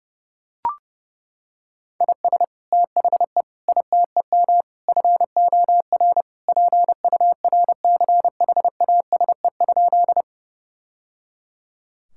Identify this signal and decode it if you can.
Morse code, heard as a keyed tone, about 30 wpm, 700 Hz. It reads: IS THE ITEM FOR PURCHASE?
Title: Sentences from Top 1000 Words - Review Part 6 - 30wpm